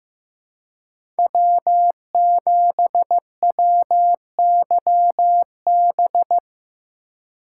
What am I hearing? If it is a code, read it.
Morse code, heard as a keyed tone, about 15 wpm, 700 Hz. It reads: W7WYB
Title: USA Call Signs - 2x Character Spacing - 15wpm